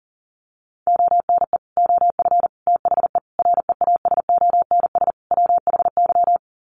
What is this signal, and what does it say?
Morse code, heard as a keyed tone, about 40 wpm, 700 Hz. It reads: ONE OF THE REASONS WHY
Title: Sets of 5 Words 40wpm